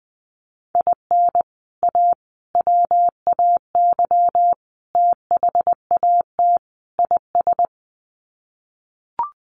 Morse code, heard as a keyed tone, about 20 wpm, 700 Hz. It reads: IN A WAY THAT IS
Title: Sets of 5 Words Encore 20wpm